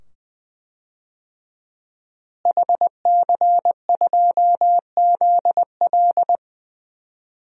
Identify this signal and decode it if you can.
Morse code, heard as a keyed tone, about 20 wpm, 700 Hz. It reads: HC2ZL
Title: DXCC-100 Call Signs - 5 Characters - Repeated 2x - 20wpm